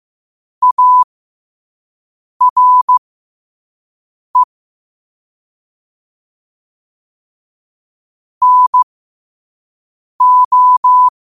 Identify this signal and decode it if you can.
Morse code, heard as a keyed tone, about 15 wpm, spaced as for 5 wpm, 1000 Hz. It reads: ARE NO
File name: qotdfiles / 5wpm